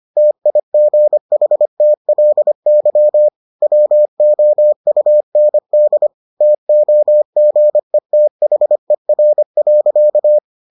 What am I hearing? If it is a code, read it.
Morse code, heard as a keyed tone, about 25 wpm, 600 Hz. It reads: TIGHTLY WOUND TOGETHER.